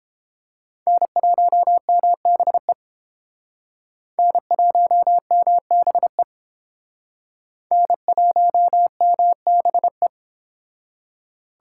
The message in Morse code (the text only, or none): N1MBE N1MBE N1MBE